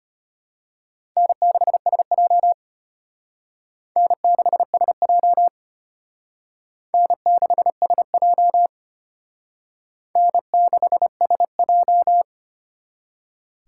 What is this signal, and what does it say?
Morse code, heard as a keyed tone, about 32 wpm, 700 Hz. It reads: N6SJ N6SJ N6SJ N6SJ